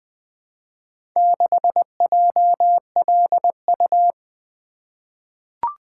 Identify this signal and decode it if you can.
Morse code, heard as a keyed tone, about 20 wpm, 700 Hz. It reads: 6JLU
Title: Four Letter-Number Combinations - 20wpm